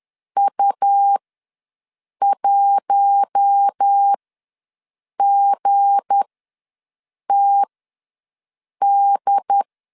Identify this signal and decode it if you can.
Morse code, heard as a keyed tone, about 10 wpm, spaced as for 6 wpm, 800 Hz. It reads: U1GTD
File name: LEZIONI / 7 GF9Z / 7 G ws 10wpm 3843